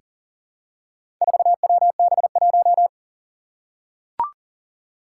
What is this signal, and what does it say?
Morse code, heard as a keyed tone, about 37 wpm, 700 Hz. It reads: 4WB1
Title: Four Challenging Letter-Number Combinations 40wpm